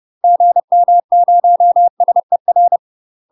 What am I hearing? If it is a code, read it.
Morse code, heard as a keyed tone, about 30 wpm, 700 Hz. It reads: GM0SER
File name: UK MASTER QSOs TRAINING / 30WPM / UK MASTER QSOs 0030